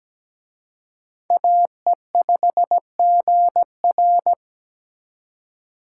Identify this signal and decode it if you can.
Morse code, heard as a keyed tone, about 20 wpm, 700 Hz. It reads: AE5GR
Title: Speed Racing - USA 2x2 Call Signs 17wpm